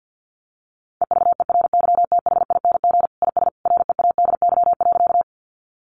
Morse code, heard as a keyed tone, about 50 wpm, 700 Hz. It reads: EVERYTHING IS READY.
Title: Sentences from Top 700 Words 50wpm